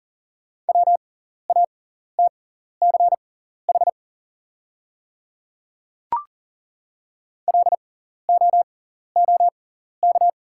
Morse code, heard as a keyed tone, about 40 wpm, spaced as for 14 wpm, 690 Hz. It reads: WATCH LOOK
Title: 6x Character Spacing - Top 300 Words - 40wpm